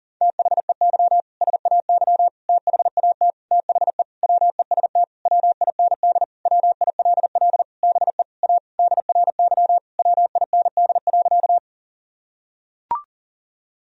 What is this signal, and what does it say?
Morse code, heard as a keyed tone, about 40 wpm, 700 Hz. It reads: THEY SAY THAT THE WEST WIND WILL BE A DRY WIND.
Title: Sentences from Top 1000 Words - Review Part 3 - 40wpm